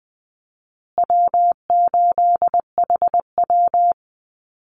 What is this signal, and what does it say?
Morse code, heard as a keyed tone, about 20 wpm, 700 Hz. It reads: W8HW
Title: Easy Call Signs Repeated 3x - 20wpm